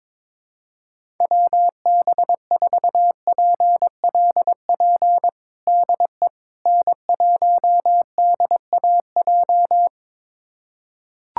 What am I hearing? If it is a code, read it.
Morse code, heard as a keyed tone, about 21 wpm, 700 Hz. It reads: WB4PLP DE N1DAJ
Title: USA Callsign DE Callsign 22wpm